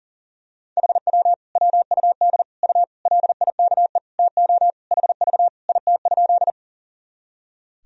Morse code, heard as a keyed tone, about 40 wpm, 700 Hz. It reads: HW WUD U LIKE TO HV IT?
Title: Sentences from Top 100 Words and Abbreviations 40wpm